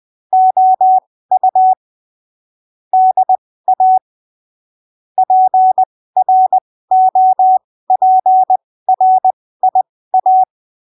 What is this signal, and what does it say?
Morse code, heard as a keyed tone, about 20 wpm, spaced as for 15 wpm, 750 Hz. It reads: OU DA PROPRIA
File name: arquivos / texto / texto 04 20ppm